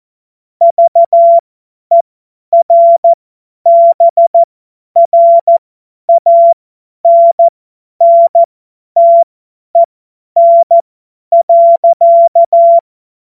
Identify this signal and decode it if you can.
Morse code, heard as a keyed tone, about 14 wpm, spaced as for 10 wpm, 680 Hz. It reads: VERBRANNTEN.
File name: Morsekurs / Witze / Witz 09. November 2021, 50 F70